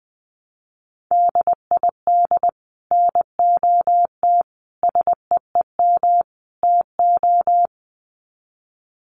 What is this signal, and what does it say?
Morse code, heard as a keyed tone, about 19 wpm, 700 Hz. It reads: DID NOT SEEM TO
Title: Sets of 4 Words 20wpm